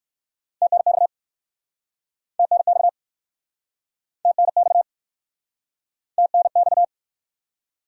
Morse code, heard as a keyed tone, about 61 wpm, 700 Hz. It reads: TNX TNX TNX TNX